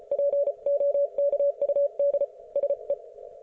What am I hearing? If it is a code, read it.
Morse code, heard as a keyed tone, about 34 wpm, 570 Hz. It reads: POKUD SE